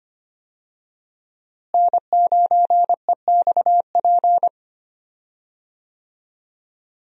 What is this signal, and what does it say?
Morse code, heard as a keyed tone, about 25 wpm, 700 Hz. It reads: N9EXP
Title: USA Call Signs - 5 Characters - Repeated 3x - 25wpm